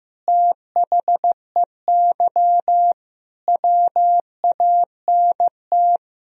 Morse code, heard as a keyed tone, about 16 wpm, 700 Hz. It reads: THEY WANT
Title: Sentences from Top 1000 Words - Review Part 3 - 15wpm